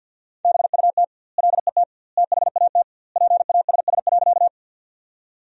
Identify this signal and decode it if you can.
Morse code, heard as a keyed tone, about 50 wpm, 700 Hz. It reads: BUT LET THAT PASS.
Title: 2x Word-Spacing - Sentences from Top 400 Words - 50wpm